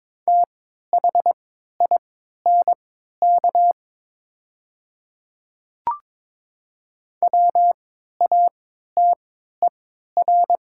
Morse code, heard as a keyed tone, about 22 wpm, spaced as for 13 wpm, 700 Hz. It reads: THINK WATER
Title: 3x Character Spacing - Top 200 Words - 22wpm